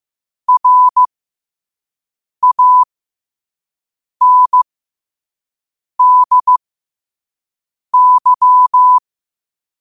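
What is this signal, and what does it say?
Morse code, heard as a keyed tone, about 15 wpm, spaced as for 5 wpm, 1000 Hz. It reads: RANDY
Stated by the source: Morse code practice at 5 wpm.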